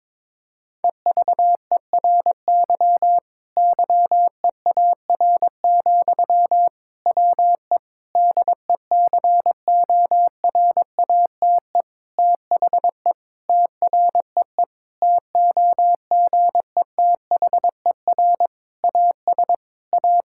Morse code, heard as a keyed tone, about 22 wpm, 700 Hz. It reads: EVERY YEAR, WE DECORATE THE TREE TOGETHER AS A
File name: Difficult Sentences with Hints - 22wpm